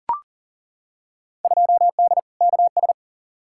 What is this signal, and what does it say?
Morse code, heard as a keyed tone, about 40 wpm, 700 Hz. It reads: 2D KS
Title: Field Day Exchanges Encore - 40wpm